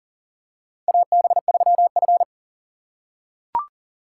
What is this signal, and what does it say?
Morse code, heard as a keyed tone, about 40 wpm, 700 Hz. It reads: AB3F
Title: USA Call Signs - 4 Characters - 3x Character Spacing - 40wpm